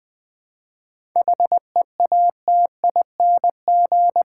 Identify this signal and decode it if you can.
Morse code, heard as a keyed tone, about 20 wpm, 700 Hz. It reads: HEATING
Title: 7 Letter Words - First 3 Letters Given - 20wpm